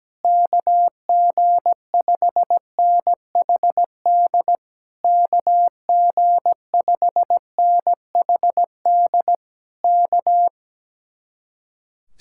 Morse code, heard as a keyed tone, about 17 wpm, 700 Hz. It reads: KG5NHD KG5NHD K
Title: Speed-Racing Common QSO Phrases - 17wpm